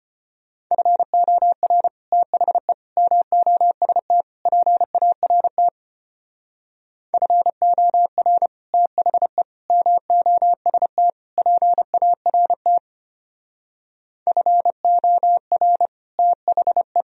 Morse code, heard as a keyed tone, about 31 wpm, 700 Hz. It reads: FOR THE MOST PART FOR THE MOST PART FOR THE
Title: Speed-Racing - Sets of 4 Words 25wpm